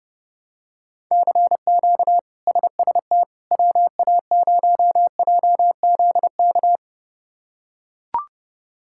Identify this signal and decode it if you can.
Morse code, heard as a keyed tone, about 30 wpm, 700 Hz. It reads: CQ SST WA0JZK
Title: K1USN SST QSO Practice - 30wpm